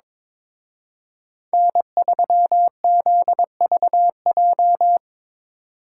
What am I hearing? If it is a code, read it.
Morse code, heard as a keyed tone, about 25 wpm, 700 Hz. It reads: N3ZVJ
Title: Speed Racing - USA 1x3 Call Signs 22wpm